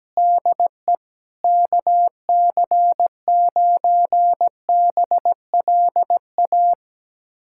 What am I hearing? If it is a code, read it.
Morse code, heard as a keyed tone, about 17 wpm, 700 Hz. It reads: DE KC9BLA
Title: USA Callsign DE Callsign 17wpm